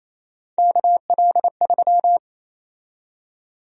Morse code, heard as a keyed tone, about 28 wpm, 700 Hz. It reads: KL3
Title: USA Call Sign Prefixes - 28wpm